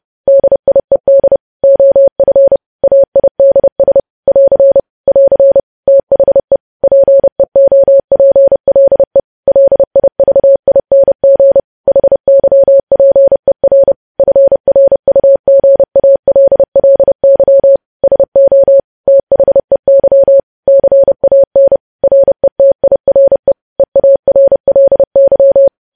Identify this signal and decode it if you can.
Morse code, heard as a keyed tone, about 30 wpm, 550 Hz. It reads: DIED OF AIDS <AR> <AR> THE PEOPLE LIVING HYPER FRUGALLY SO THEY CAN RETIRE EARLY
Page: news-headlines-in-morse-code-at-30-wpm